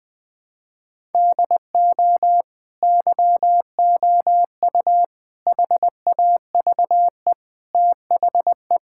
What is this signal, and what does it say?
Morse code, heard as a keyed tone, about 20 wpm, 700 Hz. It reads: DO YOU HAVE THE
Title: Sentences from Top 1000 Words - Review Part 5 - 20wpm